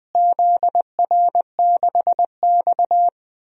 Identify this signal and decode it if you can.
Morse code, heard as a keyed tone, about 21 wpm, 700 Hz. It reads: ZR6X
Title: DXCC-100 Call Signs - 4 Characters - Repeated 3x - 20wpm